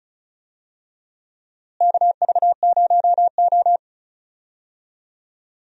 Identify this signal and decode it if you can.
Morse code, heard as a keyed tone, about 36 wpm, 700 Hz. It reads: KV0O
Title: USA Call Signs - 4 Characters - Repeated 2x - 35wpm